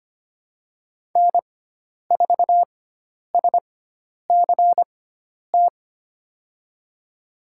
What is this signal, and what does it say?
Morse code, heard as a keyed tone, about 25 wpm, spaced as for 10 wpm, 700 Hz. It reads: N4SCT
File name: Lesson 16g - Letter c - Call Signs 25wpm